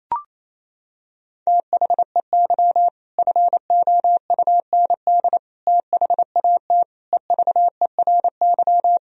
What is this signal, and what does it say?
Morse code, heard as a keyed tone, about 28 wpm, 700 Hz. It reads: THEY FOUND THAT EVERY
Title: Sentences from Top 1000 Words - Review Part 4 - 28wpm